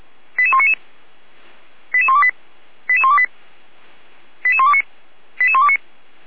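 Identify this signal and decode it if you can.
Five-tone selective call (ZVEI), heard as quick runs of five tones, tones of about 70 ms. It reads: 88199 88128 88128 88129 88129
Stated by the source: ZVEI five-tone selective call received on an SDR, logged by Feuerwehr Orlamünde fire brigade.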